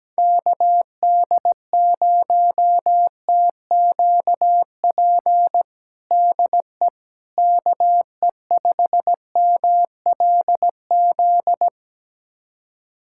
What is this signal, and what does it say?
Morse code, heard as a keyed tone, about 17 wpm, 700 Hz. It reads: KD0TQP DE KE5MLZ
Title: USA Callsign DE Callsign - 6 Characters - 17wpm